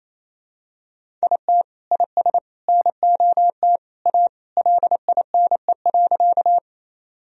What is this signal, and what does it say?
Morse code, heard as a keyed tone, about 28 wpm, 700 Hz. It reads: IT IS NOT A LINE.
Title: Sentences from Top 1000 Words - Review Part 1 - 28wpm